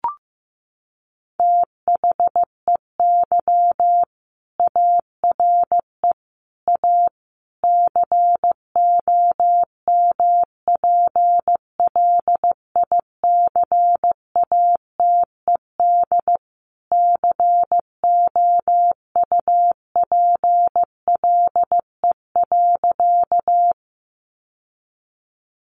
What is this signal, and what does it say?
Morse code, heard as a keyed tone, about 15 wpm, 700 Hz. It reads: THEY ARE A COMPLICATED COUPLE.
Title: Sentences from Top 1200 Words 15wpm